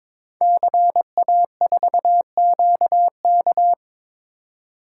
Morse code, heard as a keyed tone, about 22 wpm, 700 Hz. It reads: CA4QK
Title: Code Groups - 5 Letter:Numbers - 22wpm